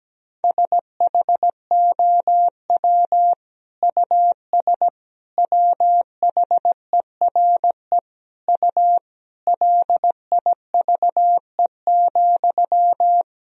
Morse code, heard as a keyed tone, about 17 wpm, 700 Hz. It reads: SHOW US WHERE U LIVE,
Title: Sentences from Top 200 Words and Abbreviations 17wpm